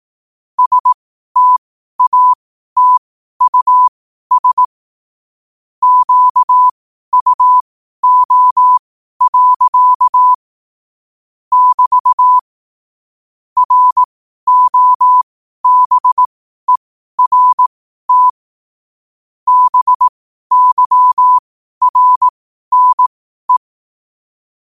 Morse code, heard as a keyed tone, about 18 wpm, spaced as for 12 wpm, 1000 Hz. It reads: STATUS QUO. <BT> ROBERT BYRNE
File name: qotdfiles / 12wpm